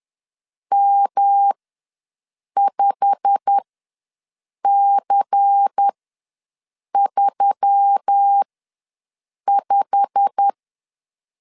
Morse code, heard as a keyed tone, about 10 wpm, spaced as for 6 wpm, 800 Hz. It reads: M5C35
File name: LEZIONI / 8 YJ45 / 8 YJ45 ws 10wpm 3843